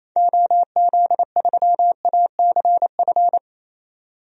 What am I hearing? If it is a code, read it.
Morse code, heard as a keyed tone, about 28 wpm, 700 Hz. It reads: OZ3ACF
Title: DXCC-100 Call Signs - 6 Characters - 28wpm